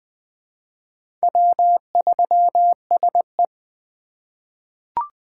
Morse code, heard as a keyed tone, about 20 wpm, 700 Hz. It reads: W3SE
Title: USA Call Signs - 4x Character Spacing - 20wpm